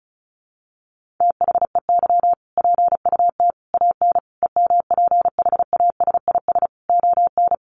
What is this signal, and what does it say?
Morse code, heard as a keyed tone, about 34 wpm, 700 Hz. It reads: THEY PUT AN EMPHASIS ON
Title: Sentences from Top 1400 Words 35wpm